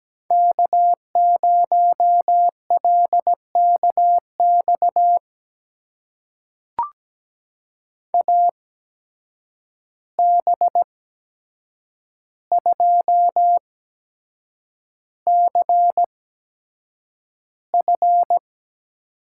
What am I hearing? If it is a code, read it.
Morse code, heard as a keyed tone, about 17 wpm, spaced as for 7 wpm, 700 Hz. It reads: K0LKX AB2CF
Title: USA Call Signs - 5 Characters - 8x Character Spacing - 17wpm